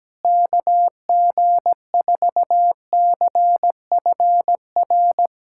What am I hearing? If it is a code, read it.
Morse code, heard as a keyed tone, about 17 wpm, 700 Hz. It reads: KG4CFR
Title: USA Callsign DE Callsign - 6 Characters - 17wpm